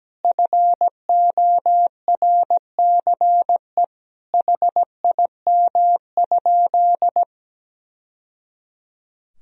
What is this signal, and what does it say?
Morse code, heard as a keyed tone, about 17 wpm, 700 Hz. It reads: FORCE HIM?
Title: Sentences from Top 1000 Words - Review Part 3 - 17wpm